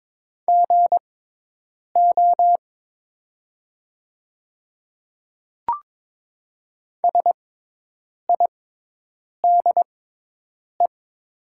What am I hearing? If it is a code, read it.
Morse code, heard as a keyed tone, about 22 wpm, spaced as for 8 wpm, 700 Hz. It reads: GO SIDE
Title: 6x Character Spacing - Top 100 Words - 22wpm